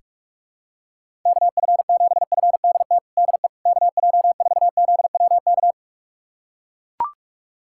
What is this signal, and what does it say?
Morse code, heard as a keyed tone, about 42 wpm, 700 Hz. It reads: KF7FDT DE KJ4ZWK
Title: USA Callsign DE Callsign - 6 Characters - 45wpm